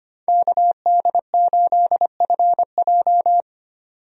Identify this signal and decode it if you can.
Morse code, heard as a keyed tone, about 25 wpm, 690 Hz. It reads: KD8FJ
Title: USA Call Signs - 5 Characters - 25wpm